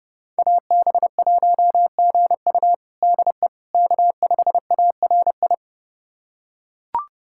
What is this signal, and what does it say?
Morse code, heard as a keyed tone, about 32 wpm, 700 Hz. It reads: AB1GU DE K5ARI
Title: USA Callsign DE Callsign - 5 Characters - 30wpm